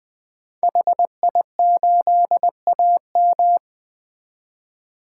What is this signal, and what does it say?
Morse code, heard as a keyed tone, about 20 wpm, 700 Hz. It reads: HI8AM
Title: 2023 POTA Activator Call Signs - 20wpm